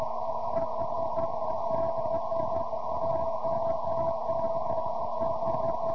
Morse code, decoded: none